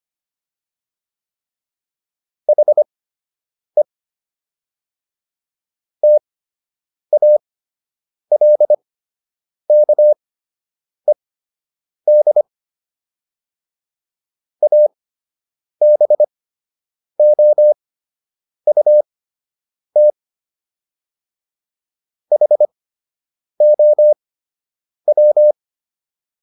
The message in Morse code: HE TALKED ABOUT HOW

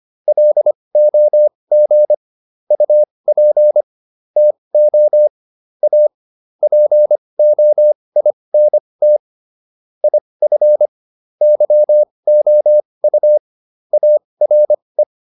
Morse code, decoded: LOG UP TO A POINT IF YOU ARE

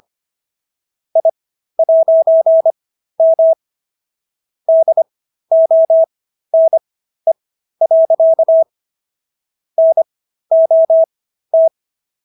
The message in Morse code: I'M DONE. NOT

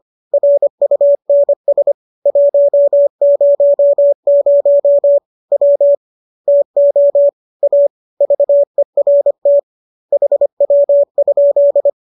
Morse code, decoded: RUNS 100 W TO A VERT HW?

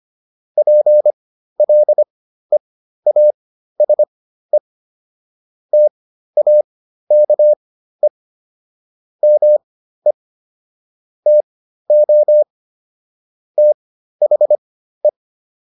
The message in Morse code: PLEASE TAKE ME TO THE

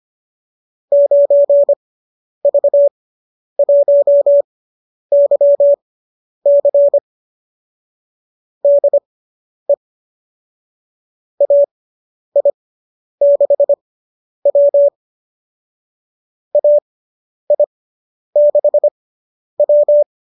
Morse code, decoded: 9V1YC DE AI6W AI6W

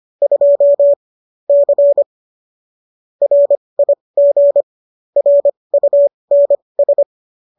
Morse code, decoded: 2 C RIG RUNS

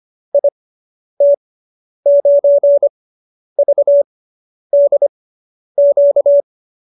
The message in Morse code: IT9VDQ

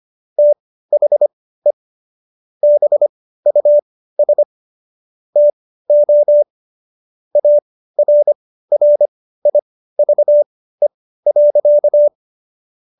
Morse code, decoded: THE BUS TO ARRIVE.

